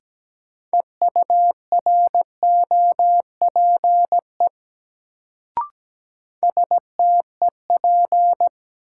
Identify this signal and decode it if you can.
Morse code, heard as a keyed tone, about 17 wpm, 700 Hz. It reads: EUROPE STEP
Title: Sets of 5 Words Encore - New Word Drill - 17wpm